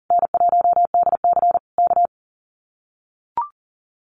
Morse code, heard as a keyed tone, about 37 wpm, 700 Hz. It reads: N1DC K
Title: Common QSO Phrases Encore 40wpm